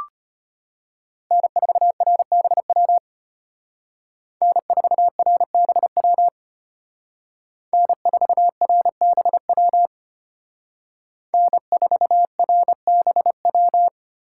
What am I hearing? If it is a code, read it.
Morse code, heard as a keyed tone, about 30 wpm, 700 Hz. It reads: N4RBW N4RBW N4RBW N4RBW
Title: Speed Racing - USA 1x3 Call Signs 25wpm